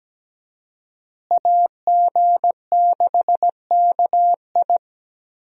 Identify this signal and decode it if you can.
Morse code, heard as a keyed tone, about 17 wpm, 700 Hz. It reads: AG6KI
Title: USA Call Signs - 5 Characters - Repeated 3x - 17wpm